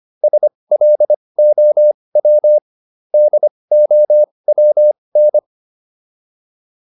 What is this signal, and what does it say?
Morse code, heard as a keyed tone, about 25 wpm, spaced as for 20 wpm, 600 Hz. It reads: SLOW DOWN
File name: PR304 20